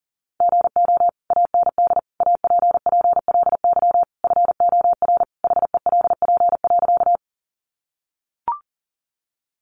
Morse code, heard as a keyed tone, about 40 wpm, 700 Hz. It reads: GO AND APPLY FOR HELP.
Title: Sentences from Top 1200 Words 40wpm